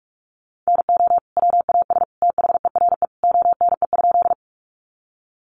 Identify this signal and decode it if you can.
Morse code, heard as a keyed tone, about 44 wpm, 700 Hz. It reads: NO WAS THERE ONE?
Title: Sentences from Top 100 Words 45wpm